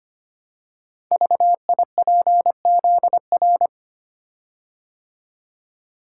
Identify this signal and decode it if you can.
Morse code, heard as a keyed tone, about 25 wpm, 700 Hz. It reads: VIPZR
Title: Code Groups - 5 Letters - 25wpm